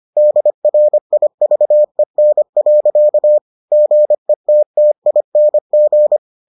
Morse code, heard as a keyed tone, about 25 wpm, 600 Hz. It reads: DRIVEN. GETTING